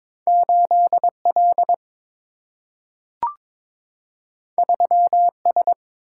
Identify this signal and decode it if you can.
Morse code, heard as a keyed tone, about 22 wpm, 700 Hz. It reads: 8 L 3 S
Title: Single-Letter Single-Number 22wpm